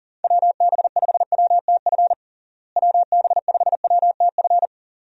Morse code, heard as a keyed tone, about 40 wpm, 700 Hz. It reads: WB5WTF WB5WTF
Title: USA Call Signs - 6 Characters - Repeated 2x - 40wpm